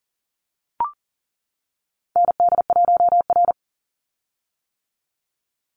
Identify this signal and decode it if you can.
Morse code, heard as a keyed tone, about 40 wpm, 700 Hz. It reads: ND1R
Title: Call Signs - 40wpm